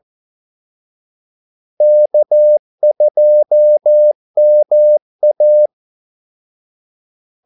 Morse code, heard as a keyed tone, about 14 wpm, 600 Hz. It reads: K2MA
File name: Callsigns-14wpm